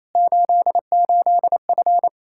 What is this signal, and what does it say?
Morse code, heard as a keyed tone, about 27 wpm, 700 Hz. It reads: 88F
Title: Practice QSO - Exchange 2 - Info - 28wpm